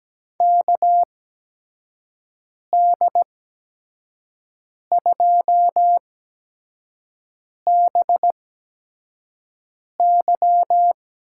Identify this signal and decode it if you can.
Morse code, heard as a keyed tone, about 17 wpm, spaced as for 5 wpm, 700 Hz. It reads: KD2BY
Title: USA Call Signs - 8x Character Spacing - 17wpm